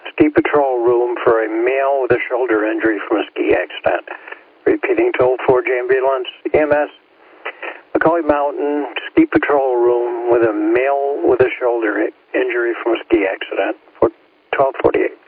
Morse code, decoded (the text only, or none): none